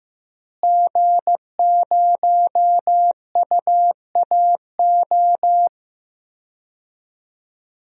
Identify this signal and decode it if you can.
Morse code, heard as a keyed tone, about 16 wpm, 700 Hz. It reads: G0UAO
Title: DXCC-100 Call Signs - 5 Characters - Repeated 2x - 15wpm